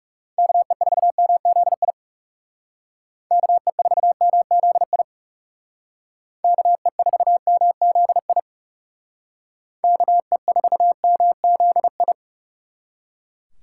Morse code, heard as a keyed tone, about 37 wpm, 700 Hz. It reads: KE4MZI KE4MZI KE4MZI KE4MZI
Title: Speed Racing - USA 2x3 Call Signs 30wpm